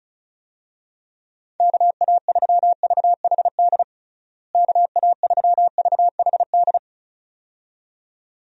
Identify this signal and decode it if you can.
Morse code, heard as a keyed tone, about 36 wpm, 700 Hz. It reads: KA3VHD KA3VHD
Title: USA Call Signs - 6 Characters - Repeated 2x - 35wpm